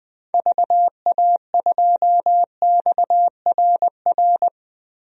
Morse code, heard as a keyed tone, about 21 wpm, 700 Hz. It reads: VA2XRR
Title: DXCC-100 Call Signs - 6 Characters - Repeated 3x - 20wpm